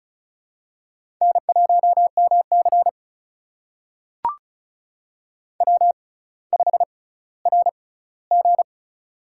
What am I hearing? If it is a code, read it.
Morse code, heard as a keyed tone, about 36 wpm, spaced as for 18 wpm, 700 Hz. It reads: N1MC W5RG
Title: USA Call Signs - 4 Characters - 6x Character Spacing - 35wpm